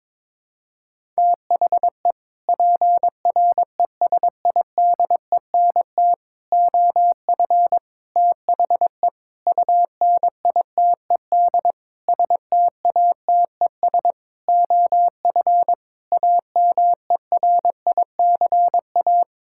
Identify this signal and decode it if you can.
Morse code, heard as a keyed tone, about 22 wpm, 700 Hz. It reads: THE PRESIDENT OF THE UNITED STATES OF AMERICA
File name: practice / 22wpm / Sets of 8 Words - 22wpm